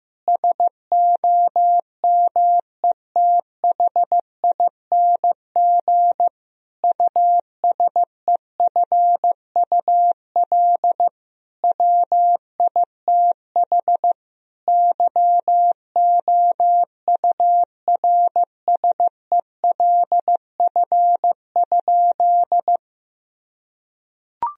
Morse code, heard as a keyed tone, about 16 wpm, 700 Hz. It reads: SOMETHING USEFUL WITH YOURSELF?
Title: Sentences from Top 1000 Words - Review Part 4 - 15wpm